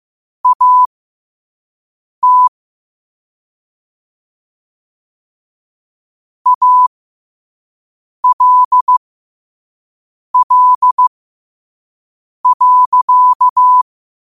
Morse code, heard as a keyed tone, about 15 wpm, spaced as for 5 wpm, 1000 Hz. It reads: AT ALL.